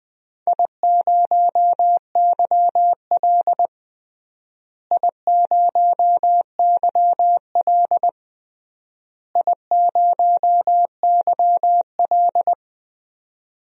Morse code, heard as a keyed tone, about 21 wpm, 700 Hz. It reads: I0YL I0YL I0YL
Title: DXCC-100 Call Signs - 4 Characters - Repeated 3x - 20wpm